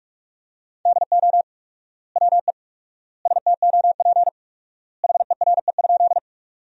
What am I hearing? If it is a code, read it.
Morse code, heard as a keyed tone, about 46 wpm, 700 Hz. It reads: DO WE STOP HERE?
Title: 4x Word-Spacing - Sentences from Top 300 Words - 45wpm